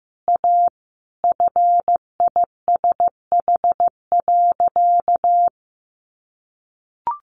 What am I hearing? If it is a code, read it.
Morse code, heard as a keyed tone, about 15 wpm, 700 Hz. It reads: A FISH.
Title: Additional Easy Phrases 15wpm